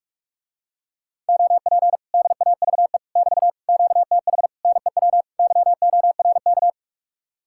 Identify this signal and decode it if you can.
Morse code, heard as a keyed tone, about 44 wpm, 700 Hz. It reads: OP DAVE <BT> QTH NEW YORK